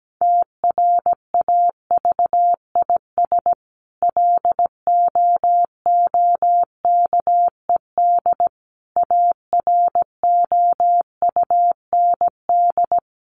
Morse code, heard as a keyed tone, about 17 wpm, 700 Hz. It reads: TRAVIS LOOKED AROUND
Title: A Sound Of Thunder 17wpm